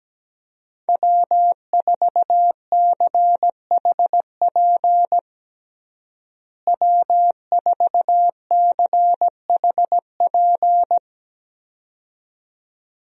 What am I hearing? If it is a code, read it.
Morse code, heard as a keyed tone, about 17 wpm, 700 Hz. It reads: W4CHP W4CHP
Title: USA Call Signs - 5 Characters - Repeated 2x - 17wpm